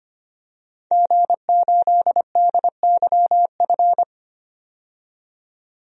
Morse code, heard as a keyed tone, about 25 wpm, 700 Hz. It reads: G8DYF
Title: DXCC-100 Call Signs - 5 Characters - 25wpm